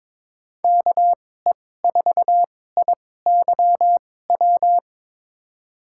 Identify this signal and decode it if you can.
Morse code, heard as a keyed tone, about 22 wpm, spaced as for 16 wpm, 700 Hz. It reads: KE4IYW
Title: USA Call Signs - 6 Characters - 2x Character Spacing - 22wpm